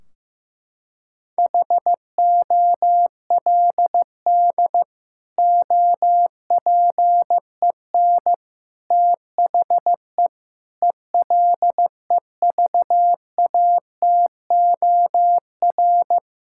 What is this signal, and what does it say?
Morse code, heard as a keyed tone, about 15 wpm, 700 Hz. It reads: HOLD OPEN THE ELEVATOR
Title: Sentences from Top 1600 Words 15wpm